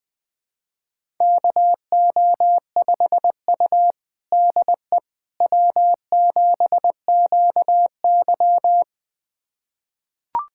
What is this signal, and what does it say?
Morse code, heard as a keyed tone, about 20 wpm, 700 Hz. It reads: KO5U DE W7QY
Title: USA Callsign DE Callsign - 4 Characters - 20wpm